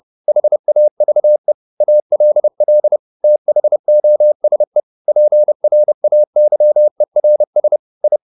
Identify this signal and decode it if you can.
Morse code, heard as a keyed tone, about 30 wpm, 600 Hz. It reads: HAVE ALL THOSE PRAYERS I